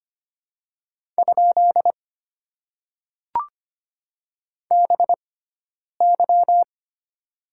Morse code, heard as a keyed tone, about 25 wpm, 700 Hz. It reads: ? B Y
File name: Top 90 QSO Elements - 6x Character Spacing - 25wpm